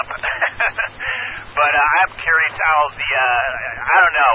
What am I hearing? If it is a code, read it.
Emergency Alert System (SAME) tones, heard as harsh buzzing bursts, with speech. It is heard from 0.5 to 4.1 s.